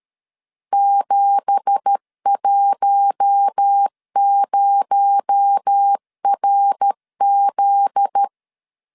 Morse code, heard as a keyed tone, about 13 wpm, 800 Hz. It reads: 710RZ
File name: CW / DET / 1 9 1500C S C 12wpm 3811